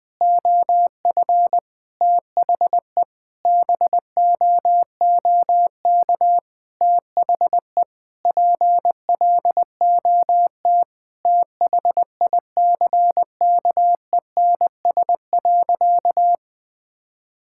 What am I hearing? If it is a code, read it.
Morse code, heard as a keyed tone, about 20 wpm, 700 Hz. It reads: OF THE BOOK THE PLOT THICKENS.
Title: Sentences with 5-Word Phrases - 20wpm